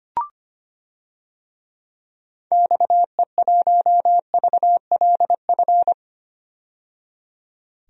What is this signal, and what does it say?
Morse code, heard as a keyed tone, about 25 wpm, 690 Hz. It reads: XE1VLF
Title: DXCC-100 Call Signs - 6 Characters - 25wpm